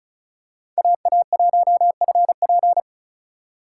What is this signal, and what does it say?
Morse code, heard as a keyed tone, about 34 wpm, 700 Hz. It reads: AA1FP
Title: USA Call Signs - 5 Characters - 4x Character Spacing - 35wpm